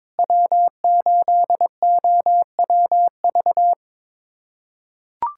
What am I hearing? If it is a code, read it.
Morse code, heard as a keyed tone, about 22 wpm, 700 Hz. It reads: W8OWV
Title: USA Callsign DE Callsign - 5 Characters - 22wpm